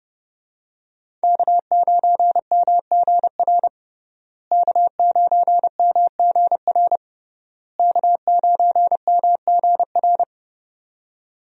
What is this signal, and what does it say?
Morse code, heard as a keyed tone, about 31 wpm, 700 Hz. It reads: K9MGR K9MGR K9MGR
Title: USA Call Signs - 5 Characters - Repeated 3x - 30wpm